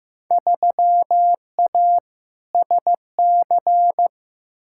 Morse code, heard as a keyed tone, about 16 wpm, 700 Hz. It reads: 3A SC
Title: Field Day Exchanges Encore - 15wpm